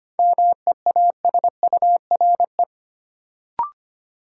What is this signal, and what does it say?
Morse code, heard as a keyed tone, about 25 wpm, 700 Hz. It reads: MEASURE
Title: Sentences from Top 1000 Words - Review Part 1 - New Word Drill - 25wpm